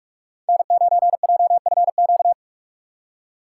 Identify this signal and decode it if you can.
Morse code, heard as a keyed tone, about 54 wpm, 700 Hz. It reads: N9JFQ
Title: Speed Racing - USA 1x3 Call Signs 45wpm